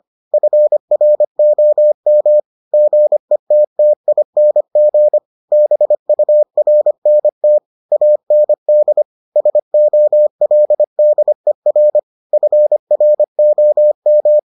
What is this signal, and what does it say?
Morse code, heard as a keyed tone, about 25 wpm, 600 Hz. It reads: FROM GETTING BURNT AND SOLDER FROM